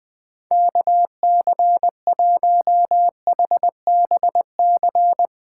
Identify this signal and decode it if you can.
Morse code, heard as a keyed tone, about 21 wpm, 700 Hz. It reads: KC1HBC